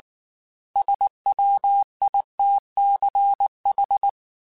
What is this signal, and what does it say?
Morse code, heard as a keyed tone, about 19 wpm, 800 Hz. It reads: SWITCH
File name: Chuckles / Chuckles-mp3s / MiddleAge @18